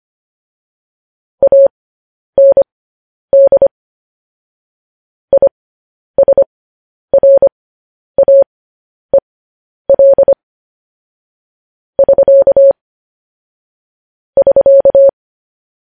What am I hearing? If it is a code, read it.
Morse code, heard as a keyed tone, about 25 wpm, spaced as for 10 wpm, 550 Hz. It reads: AND ISRAEL <SK> <SK>